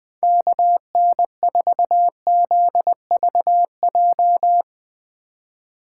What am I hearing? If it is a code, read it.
Morse code, heard as a keyed tone, about 21 wpm, 700 Hz. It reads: KN4ZVJ
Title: 2023 POTA Hunter Call Signs - 20wpm